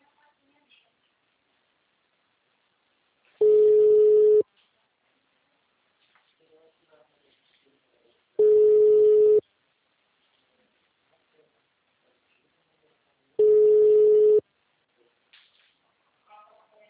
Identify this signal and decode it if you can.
Ringback tone (Europe, 425 Hz), with speech.